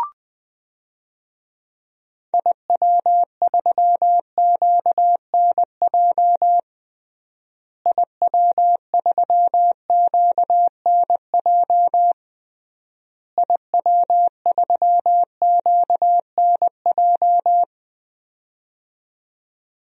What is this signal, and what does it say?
Morse code, heard as a keyed tone, about 20 wpm, 700 Hz. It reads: IW3QNJ IW3QNJ IW3QNJ
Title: DXCC-100 Call Signs - 6 Characters - Repeated 3x - 20wpm